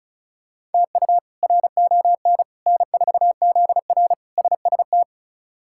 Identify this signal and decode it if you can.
Morse code, heard as a keyed tone, about 34 wpm, 700 Hz. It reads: TU RON N4ZR SST
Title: K1USN SST QSO Practice - 35wpm